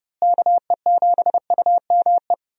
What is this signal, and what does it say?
Morse code, heard as a keyed tone, about 32 wpm, 700 Hz. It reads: KE7UME
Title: USA Call Signs - 6 Characters - Repeated 2x - 30wpm